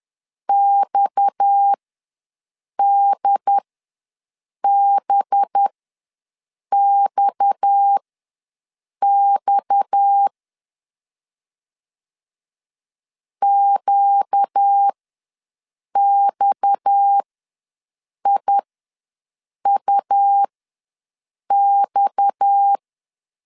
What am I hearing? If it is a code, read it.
Morse code, heard as a keyed tone, about 10 wpm, spaced as for 6 wpm, 800 Hz. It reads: XDBXX QXIUX
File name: LEZIONI / 6 PKX0 / 6 PKX ws 10wpm 3706